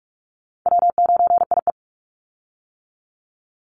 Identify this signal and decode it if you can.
Morse code, heard as a keyed tone, about 44 wpm, 700 Hz. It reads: W9IE
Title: Easy Call Signs - 45wpm